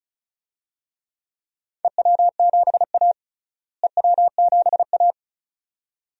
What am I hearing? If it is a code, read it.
Morse code, heard as a keyed tone, about 34 wpm, 700 Hz. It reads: EW7A EW7A
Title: DXCC-100 Call Signs - 4 Characters - Repeated 2x - 35wpm